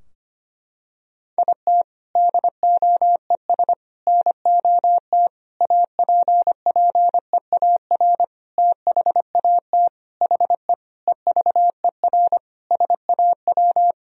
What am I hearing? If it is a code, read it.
Morse code, heard as a keyed tone, about 25 wpm, 700 Hz. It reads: IT DOES NOT APPEAR THAT HE EVER SAW
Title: Sentences from Top 1000 Words - Review Part 2 - 25wpm